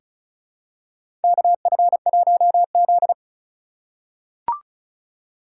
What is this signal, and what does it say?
Morse code, heard as a keyed tone, about 34 wpm, 700 Hz. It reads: KF1Z
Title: USA Call Signs - 2x Character Spacing - 35wpm